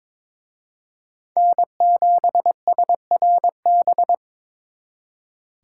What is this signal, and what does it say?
Morse code, heard as a keyed tone, about 26 wpm, 700 Hz. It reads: N7SRB
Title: Speed Racing - USA 1x3 Call Signs 22wpm